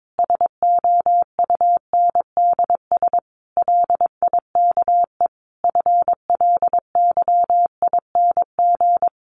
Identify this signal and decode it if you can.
Morse code, heard as a keyed tone, about 22 wpm, 700 Hz. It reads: SOUNDS LIKE FLYING
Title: A Sound Of Thunder 22wpm